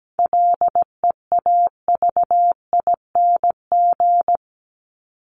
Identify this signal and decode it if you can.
Morse code, heard as a keyed tone, about 17 wpm, 700 Hz. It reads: LEAVING
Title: ING Suffix - Top 100 Words with Base Words 17wpm